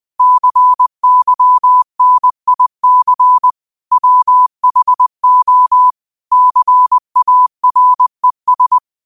Morse code, heard as a keyed tone, about 20 wpm, 1000 Hz. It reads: CYNIC WHO CARES
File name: 2024-09-13-20wpm